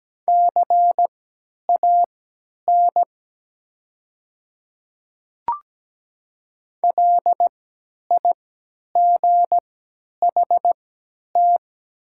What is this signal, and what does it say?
Morse code, heard as a keyed tone, about 17 wpm, spaced as for 10 wpm, 700 Hz. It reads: CAN LIGHT
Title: 3x Character Spacing - Top 200 Words - 17wpm